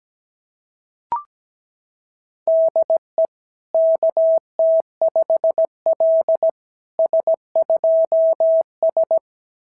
Morse code, heard as a keyed tone, about 17 wpm, 660 Hz. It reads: DE KT5L S2S
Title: SOTA QSO Practice 17wpm